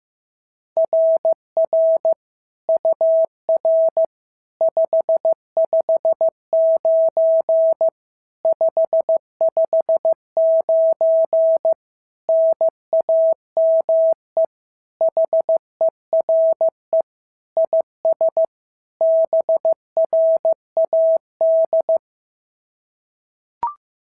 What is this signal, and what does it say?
Morse code, heard as a keyed tone, about 14 wpm, 650 Hz. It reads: RR UR 559 559 NAME HERE IS BRAD
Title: SOTA QSO Practice 15wpm